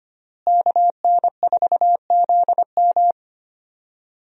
Morse code, heard as a keyed tone, about 25 wpm, 700 Hz. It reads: KN4ZM
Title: USA Call Signs - 5 Characters - Repeated 2x - 25wpm